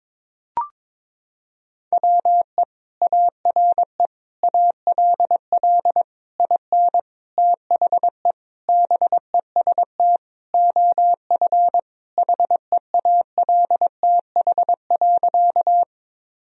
Morse code, heard as a keyed tone, about 21 wpm, 700 Hz. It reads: WE ARE ALL IN THE BEST OF HEALTH.
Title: Sentences from Top 1000 Words - Review Part 3 - 22wpm